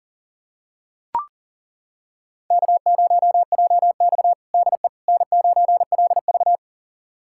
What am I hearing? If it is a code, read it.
Morse code, heard as a keyed tone, about 40 wpm, 700 Hz. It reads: K0JX DE N9LV
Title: USA Callsign DE Callsign - 4 Characters - 40wpm